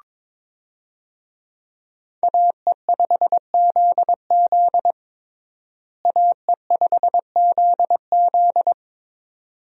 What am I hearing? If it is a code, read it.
Morse code, heard as a keyed tone, about 22 wpm, 700 Hz. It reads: AE5ZZ AE5ZZ
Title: USA Call Signs - 5 Characters - Repeated 2x - 22wpm